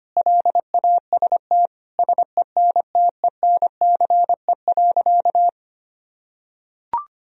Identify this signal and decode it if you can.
Morse code, heard as a keyed tone, about 25 wpm, 700 Hz. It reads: LAST SENTENCE.
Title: Sentences from Top 1000 Words - Review Part 1 - 25wpm